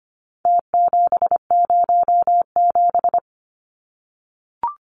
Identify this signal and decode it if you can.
Morse code, heard as a keyed tone, about 25 wpm, 700 Hz. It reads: T707
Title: Numbers - Contest Style - 4-Digit 25wpm